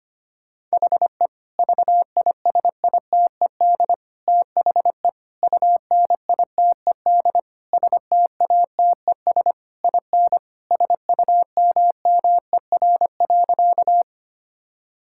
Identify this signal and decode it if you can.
Morse code, heard as a keyed tone, about 25 wpm, 700 Hz. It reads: HE VISITED THE UNITED STATES IN SUMMER.
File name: practice / 25wpm / Sentences with 4-Word Phrases - 25wpm